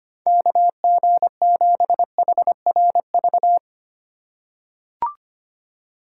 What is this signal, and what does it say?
Morse code, heard as a keyed tone, about 25 wpm, 690 Hz. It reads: KG7HRV
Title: USA Call Signs - 6 Characters - 2x Character Spacing - 25wpm